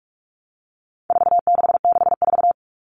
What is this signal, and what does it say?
Morse code, heard as a keyed tone, about 44 wpm, 700 Hz. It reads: 4664
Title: All Cut Numbers - 4-Digit 45wpm